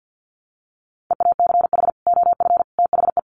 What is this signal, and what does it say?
Morse code, heard as a keyed tone, about 50 wpm, 700 Hz. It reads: EACH OF THE
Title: Sets of 3 Words 50wpm